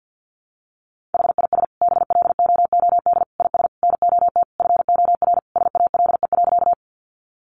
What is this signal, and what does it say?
Morse code, heard as a keyed tone, about 50 wpm, 700 Hz. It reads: HIS BLOOD IS NOT FOR SALE.
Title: Sentences from Top 1400 Words 50wpm